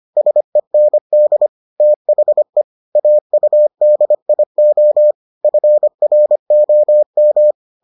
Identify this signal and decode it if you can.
Morse code, heard as a keyed tone, about 25 wpm, 600 Hz. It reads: SEND THE AUDIO FROM